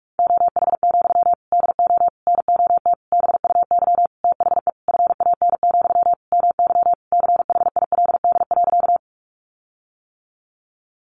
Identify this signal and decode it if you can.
Morse code, heard as a keyed tone, about 44 wpm, 700 Hz. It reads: OH, DO NOT BUY THE FAN, MY CHILD.
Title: Sentences from Top 1300 Words 45wpm